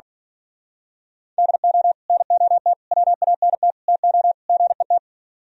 Morse code, heard as a keyed tone, about 42 wpm, 700 Hz. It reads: DO NOT WANT TO GET
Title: Speed-Racing - Sets of 5 Words 35wpm